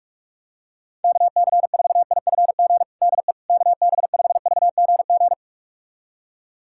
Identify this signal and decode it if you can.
Morse code, heard as a keyed tone, about 46 wpm, 700 Hz. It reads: KC4IFG DE KB5VGG
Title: USA Callsign DE Callsign - 6 Characters - 45wpm